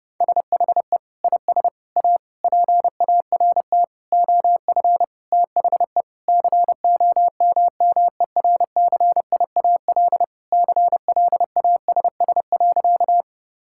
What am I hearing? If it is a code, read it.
Morse code, heard as a keyed tone, about 28 wpm, 700 Hz. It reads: SHE IS A PART OF THE COMMERCIAL CLASS.
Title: Sentences from Top 1000 Words - Review Part 5 - 30wpm